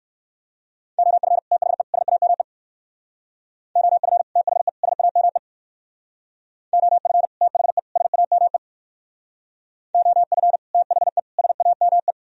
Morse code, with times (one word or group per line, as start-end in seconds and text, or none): 1.0-1.4 OF
1.5-1.8 THE
1.9-2.4 SAME
3.8-4.2 OF
4.4-4.7 THE
4.8-5.4 SAME
6.7-7.3 OF
7.4-7.8 THE
8.0-8.6 SAME
9.9-10.6 OF
10.7-11.2 THE
11.4-12.1 SAME